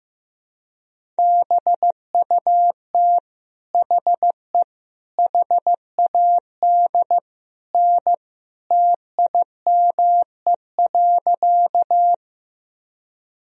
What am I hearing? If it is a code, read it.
Morse code, heard as a keyed tone, about 15 wpm, 700 Hz. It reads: BUT HE HAD N TIME.
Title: Sentences from Top 100 Words and Abbreviations 15wpm